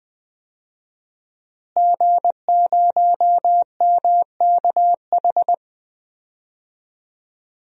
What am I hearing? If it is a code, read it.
Morse code, heard as a keyed tone, about 21 wpm, 700 Hz. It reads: G0MKH
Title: DXCC-100 Call Signs - 5 Characters - Repeated 2x - 20wpm